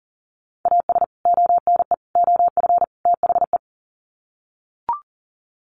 Morse code, heard as a keyed tone, about 40 wpm, 700 Hz. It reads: AS ONE OF THE
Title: Sets of 4 Words 40wpm